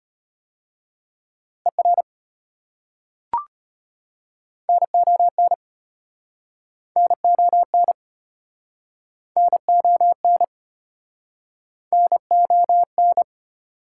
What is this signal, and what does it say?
Morse code, heard as a keyed tone, about 33 wpm, 700 Hz. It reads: ER NON NON NON NON